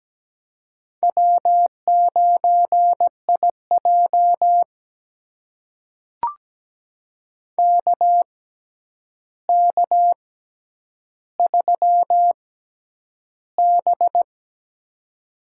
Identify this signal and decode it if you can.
Morse code, heard as a keyed tone, about 17 wpm, spaced as for 9 wpm, 700 Hz. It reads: W9IJ KK3B
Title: USA Call Signs - 4 Characters - 6x Character Spacing - 17wpm